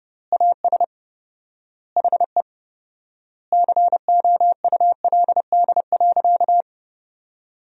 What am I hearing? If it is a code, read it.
Morse code, heard as a keyed tone, about 30 wpm, 700 Hz. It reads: AS HE COULD.
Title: 4x Word-Spacing - Sentences from Top 500 Words - 30wpm